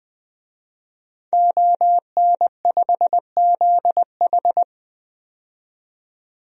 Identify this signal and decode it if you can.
Morse code, heard as a keyed tone, about 20 wpm, 700 Hz. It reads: ON5ZH
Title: DXCC-100 Call Signs - 5 Characters - 20wpm